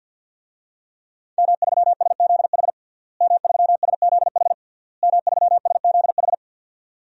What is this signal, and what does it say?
Morse code, heard as a keyed tone, about 47 wpm, 700 Hz. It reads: M3SZH M3SZH M3SZH
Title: DXCC-100 Call Signs - 5 Characters - Repeated 3x - 50wpm